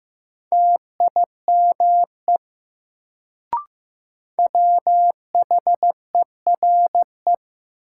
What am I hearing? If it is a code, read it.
Morse code, heard as a keyed tone, about 15 wpm, 700 Hz. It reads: TIME WHERE
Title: Sets of 2 Words - New Word Drill - 15wpm